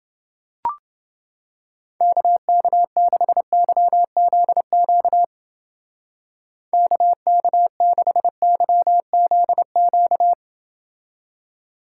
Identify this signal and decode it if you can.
Morse code, heard as a keyed tone, about 30 wpm, 700 Hz. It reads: KK6YZQ KK6YZQ